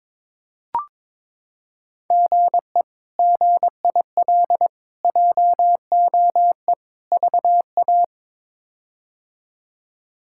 Morse code, heard as a keyed tone, about 23 wpm, 700 Hz. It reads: GE GIL JOE VA
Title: K1USN SST QSO Practice - 22wpm